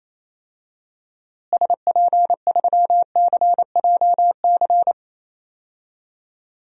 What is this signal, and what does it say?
Morse code, heard as a keyed tone, about 28 wpm, 700 Hz. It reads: SP3CJC
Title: DXCC-100 Call Signs - 6 Characters - Repeated 3x - 28wpm